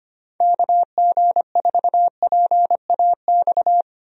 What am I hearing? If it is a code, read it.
Morse code, heard as a keyed tone, about 25 wpm, 700 Hz. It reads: KG4PAX
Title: USA Call Signs - 6 Characters - Repeated 2x - 25wpm